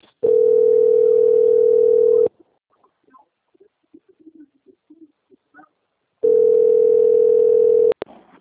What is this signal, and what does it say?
Ringback tone (North America), 440 + 480 Hz.